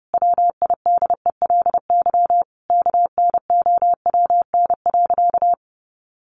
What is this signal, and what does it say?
Morse code, heard as a keyed tone, about 30 wpm, 700 Hz. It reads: WIDELY KNOWN.
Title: Sentences from Top 1200 Words 30wpm